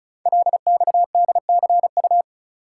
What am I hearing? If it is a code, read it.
Morse code, heard as a keyed tone, about 36 wpm, 700 Hz. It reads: LXDCU
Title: Code Groups - 5 Letters - 35wpm